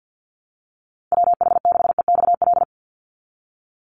Morse code, heard as a keyed tone, about 50 wpm, 700 Hz. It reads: WH6EXL
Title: USA Call Signs 50wpm